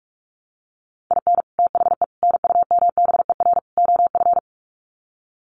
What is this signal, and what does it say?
Morse code, heard as a keyed tone, about 44 wpm, 700 Hz. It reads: IN THE NUMBER OF
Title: Sets of 4 Words 45wpm